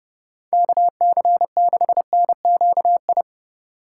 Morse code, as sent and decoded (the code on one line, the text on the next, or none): -.- -.-. -.... -. --.- ..
KC6NQI